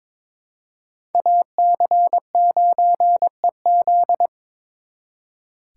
Morse code, heard as a keyed tone, about 26 wpm, 700 Hz. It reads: AC9EZ
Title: Speed Racing - USA Call Signs 22wpm